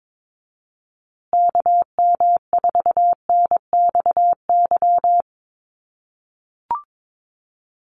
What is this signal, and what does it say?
Morse code, heard as a keyed tone, about 22 wpm, 700 Hz. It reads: KM4NXY
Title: USA Call Signs Repeated 2x 22wpm